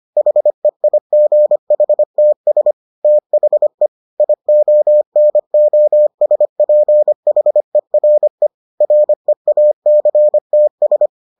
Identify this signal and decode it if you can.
Morse code, heard as a keyed tone, about 25 wpm, 600 Hz. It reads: HEIGHTS THE IONOSPHERE REACTS